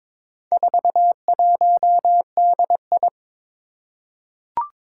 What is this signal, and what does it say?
Morse code, heard as a keyed tone, about 22 wpm, 700 Hz. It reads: 41DI
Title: Four Challenging Letter-Number Combinations 22wpm